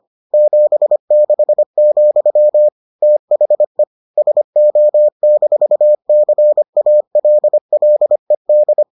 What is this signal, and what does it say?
Morse code, heard as a keyed tone, about 25 wpm, 600 Hz. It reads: 76, THE SO-CALLED